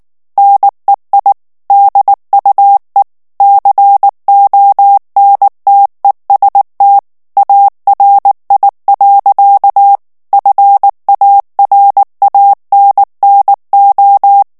Morse code, heard as a keyed tone, about 19 wpm, 800 Hz. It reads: NEI DUE CONTEST ARI. FARANNO
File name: morse mp3 / ITALIEN texte clair 2 vitesse 18 mots minute